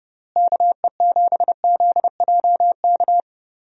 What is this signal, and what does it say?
Morse code, heard as a keyed tone, about 32 wpm, 700 Hz. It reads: KE7ZJK
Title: USA Call Signs - 6 Characters - Repeated 2x - 30wpm